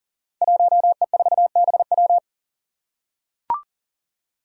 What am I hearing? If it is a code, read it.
Morse code, heard as a keyed tone, about 40 wpm, 700 Hz. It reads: 1E4BW
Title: Five Challenging Letter-Number Combinations 40wpm